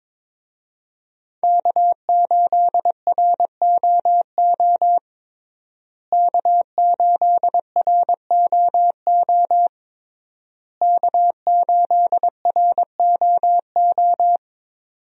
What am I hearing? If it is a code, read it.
Morse code, heard as a keyed tone, about 22 wpm, 700 Hz. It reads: K8ROO K8ROO K8ROO
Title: USA Call Signs - 5 Characters - Repeated 3x - 22wpm